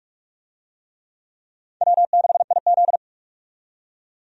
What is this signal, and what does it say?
Morse code, heard as a keyed tone, about 41 wpm, 700 Hz. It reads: W6IZ
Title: USA Call Signs - 4 Characters - 45wpm